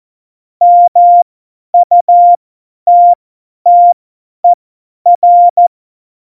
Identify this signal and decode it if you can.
Morse code, heard as a keyed tone, about 14 wpm, spaced as for 10 wpm, 700 Hz. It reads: MUTTER
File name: Morsekurs / Witze / Witz 03.Juli 2018, 50 F70